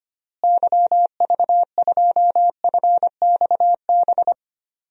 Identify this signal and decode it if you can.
Morse code, heard as a keyed tone, about 25 wpm, 700 Hz. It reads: YV2FXB
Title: DXCC-100 Call Signs - 6 Characters - Repeated 3x - 25wpm